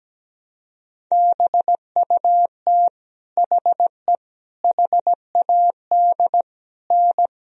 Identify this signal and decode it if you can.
Morse code, heard as a keyed tone, about 17 wpm, 700 Hz. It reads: BUT HE HAD N
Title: Sentences from Top 100 Words and Abbreviations 17wpm